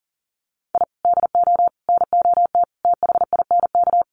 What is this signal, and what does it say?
Morse code, heard as a keyed tone, about 40 wpm, 700 Hz. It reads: I DO NOT THINK
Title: Sets of 5 Words 40wpm